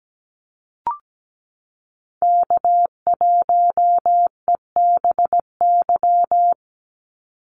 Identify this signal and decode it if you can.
Morse code, heard as a keyed tone, about 17 wpm, 700 Hz. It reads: K1EBY
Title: Call Signs Repeated 2x - 17wpm